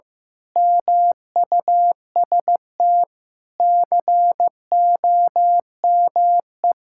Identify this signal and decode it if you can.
Morse code, heard as a keyed tone, about 15 wpm, 700 Hz. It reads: MUST COME
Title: Sentences from Top 1000 Words - Review Part 6 - 15wpm